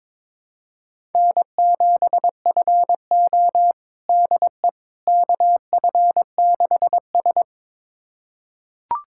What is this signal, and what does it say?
Morse code, heard as a keyed tone, about 22 wpm, 700 Hz. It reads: N7FO DE KF6S
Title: USA Callsign DE Callsign - 4 Characters - 22wpm